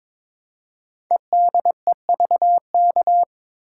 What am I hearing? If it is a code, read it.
Morse code, heard as a keyed tone, about 22 wpm, 700 Hz. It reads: EDEVK
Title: Code Groups - 5 Letters - 22wpm